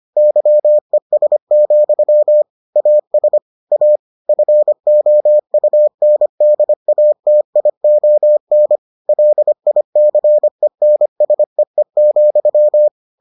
Morse code, heard as a keyed tone, about 25 wpm, 600 Hz. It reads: YES, AS A FOUNDATION LICENSEE,